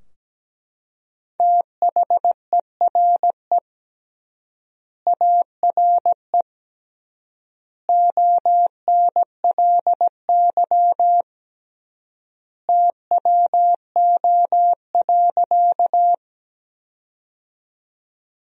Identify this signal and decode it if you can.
Morse code, heard as a keyed tone, about 17 wpm, 700 Hz. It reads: THERE ARE ONLY TWO.
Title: 3x Word-Spacing - Sentences from Top 200 Words - 17wpm